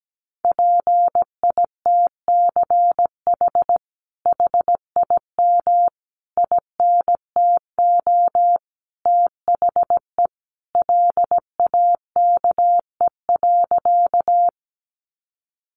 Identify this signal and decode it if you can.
Morse code, heard as a keyed tone, about 17 wpm, 700 Hz. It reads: PITCH HIM INTO THE LAKE.
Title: Sentences from Top 1300 Words 17wpm